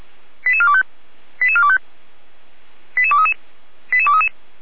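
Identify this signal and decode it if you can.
Five-tone selective call (ZVEI), heard as quick runs of five tones, tones of about 70 ms. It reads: 88425 88425 88233 88233
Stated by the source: ZVEI five-tone call 88393 received on SDR.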